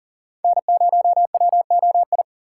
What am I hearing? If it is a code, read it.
Morse code, heard as a keyed tone, about 40 wpm, 700 Hz. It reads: N0WOI
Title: K1USN SST QSO Practice - 40wpm